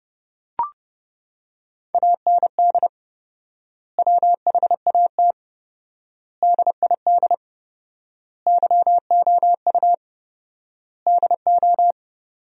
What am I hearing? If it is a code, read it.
Morse code, heard as a keyed tone, about 32 wpm, 700 Hz. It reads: AND WHAT DID YOU DO
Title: 4x Word-Spacing - Sentences from Top 100 Words - 30wpm